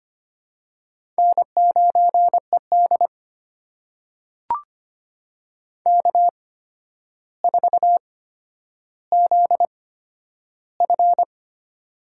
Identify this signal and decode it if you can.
Morse code, heard as a keyed tone, about 25 wpm, spaced as for 11 wpm, 700 Hz. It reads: N9ED K4ZF
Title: USA Call Signs - 4 Characters - 8x Character Spacing - 25wpm